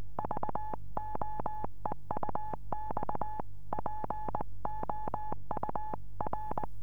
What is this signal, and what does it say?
Morse code, heard as a keyed tone, about 21 wpm, 850 Hz. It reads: VOEUX POUR
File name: 432 eme / f6eti-dl9kr 432